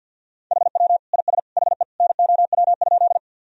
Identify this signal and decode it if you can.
Morse code, heard as a keyed tone, about 51 wpm, 700 Hz. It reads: HW IS HE NOW?
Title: Sentences from Top 100 Words and Abbreviations 50wpm